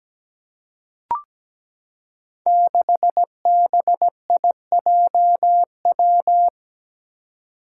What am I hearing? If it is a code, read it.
Morse code, heard as a keyed tone, about 17 wpm, 700 Hz. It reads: 6BIJW
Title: Five Challenging Letter-Number Combinations 17wpm